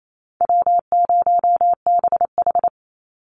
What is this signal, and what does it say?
Morse code, heard as a keyed tone, about 28 wpm, 700 Hz. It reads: W0BH
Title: Call Signs - 28wpm